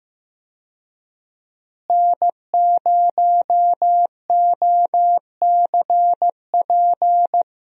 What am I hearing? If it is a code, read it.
Morse code, heard as a keyed tone, about 15 wpm, 700 Hz. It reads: N0OCP
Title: USA Call Signs - 5 Characters - Repeated 3x - 15wpm